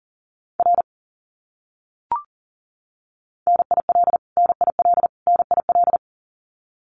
Morse code, heard as a keyed tone, about 40 wpm, 700 Hz. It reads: R NIL NIL NIL
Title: Abbreviations Repeated 3x - 40wpm